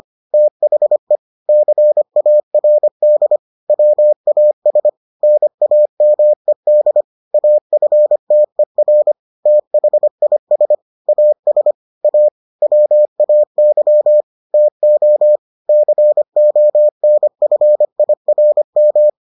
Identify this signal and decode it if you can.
Morse code, heard as a keyed tone, about 25 wpm, 600 Hz. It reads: THE CARD WAS NAMED AFTER THIS AS A WAY TO CONFIRM